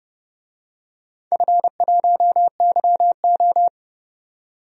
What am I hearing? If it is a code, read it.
Morse code, heard as a keyed tone, about 30 wpm, 700 Hz. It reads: F1YO
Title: DXCC-100 Call Signs - 4 Characters - 30wpm